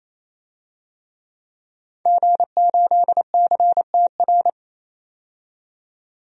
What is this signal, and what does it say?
Morse code, heard as a keyed tone, about 28 wpm, 700 Hz. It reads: G8CTR
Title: DXCC-100 Call Signs - 5 Characters - 28wpm